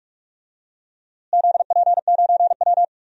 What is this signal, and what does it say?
Morse code, heard as a keyed tone, about 41 wpm, 700 Hz. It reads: ZP9W
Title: DXCC-100 Call Signs - 4 Characters - Repeated 3x - 45wpm